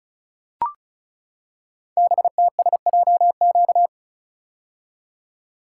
Morse code, heard as a keyed tone, about 36 wpm, 700 Hz. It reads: BTSJQ
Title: Code Groups - 5 Letters - 35wpm